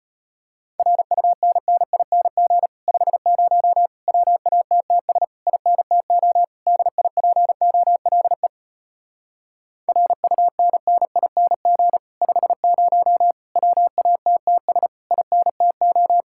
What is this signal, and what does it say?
Morse code, heard as a keyed tone, about 33 wpm, 700 Hz. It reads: RUNNING 50 WATTS INTO DIPOLE RUNNING 50 WATTS INTO